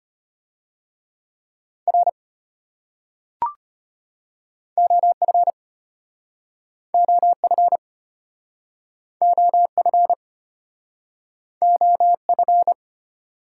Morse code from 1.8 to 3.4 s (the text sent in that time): R